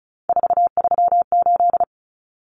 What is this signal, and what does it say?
Morse code, heard as a keyed tone, about 34 wpm, 700 Hz. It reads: 438
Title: Common Cut Numbers - 3-Digit 35wpm